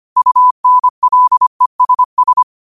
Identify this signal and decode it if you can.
Morse code, heard as a keyed tone, about 25 wpm, 1000 Hz. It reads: UNLESS
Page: New quotes every day in morse code at 25 Words per minute.